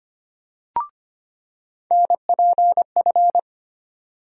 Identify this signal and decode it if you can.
Morse code, heard as a keyed tone, about 25 wpm, 700 Hz. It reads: NPF
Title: USA Call Sign Postfixes - 3 Letters 25wpm